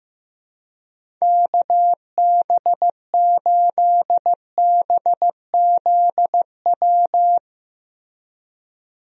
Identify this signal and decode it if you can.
Morse code, heard as a keyed tone, about 16 wpm, 700 Hz. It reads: KB8BZW
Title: USA Call Signs - 6 Characters - Repeated 3x - 15wpm